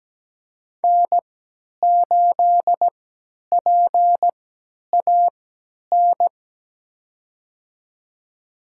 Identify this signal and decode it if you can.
Morse code, heard as a keyed tone, about 17 wpm, spaced as for 10 wpm, 700 Hz. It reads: N8PAN
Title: USA Call Signs - 3x Character Spacing - 17wpm